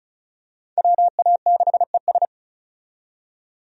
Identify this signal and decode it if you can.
Morse code, heard as a keyed tone, about 36 wpm, 700 Hz. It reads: WA6ES
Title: USA Call Signs - 5 Characters - 35wpm